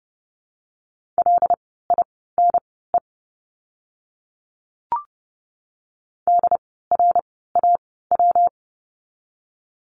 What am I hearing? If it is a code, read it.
Morse code, heard as a keyed tone, about 30 wpm, spaced as for 17 wpm, 700 Hz. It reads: LINE DRAW
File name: course / 30:10wpm / 115 Lesson 19f - Letter w - Words 30wpm